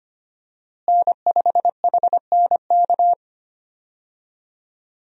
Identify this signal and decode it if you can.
Morse code, heard as a keyed tone, about 25 wpm, 700 Hz. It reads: N5HNK
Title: USA Call Signs - 5 Characters - Repeated 2x - 25wpm